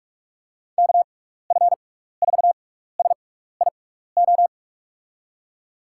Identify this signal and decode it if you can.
Morse code, heard as a keyed tone, about 46 wpm, spaced as for 16 wpm, 700 Hz. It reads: KF4SIO
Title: USA Call Signs - 6 Characters - 6x Character Spacing - 45wpm